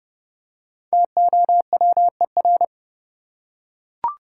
Morse code, heard as a keyed tone, about 29 wpm, 700 Hz. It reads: TOWER
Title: Five Letter Words - 30wpm